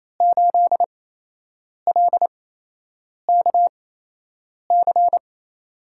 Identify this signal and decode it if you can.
Morse code, heard as a keyed tone, about 29 wpm, 700 Hz. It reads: 8 L K C
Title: USA Call Signs - 6 Characters - 8x Character Spacing - 28wpm